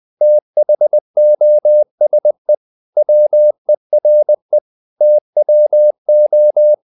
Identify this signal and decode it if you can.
Morse code, heard as a keyed tone, about 20 wpm, 600 Hz. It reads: THOSE WERE TWO
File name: the old man and the sea part 2 20wpm